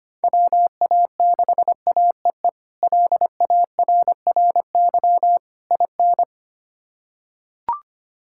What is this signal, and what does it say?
Morse code, heard as a keyed tone, about 25 wpm, 700 Hz. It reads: WA6AEE LARRY IN
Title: K1USN SST QSO Practice - 25wpm